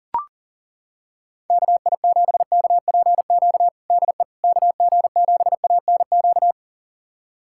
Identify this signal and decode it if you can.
Morse code, heard as a keyed tone, about 40 wpm, 700 Hz. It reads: KI7KPQ DE KG7ANQ
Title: USA Callsign DE Callsign - 6 Characters - 40wpm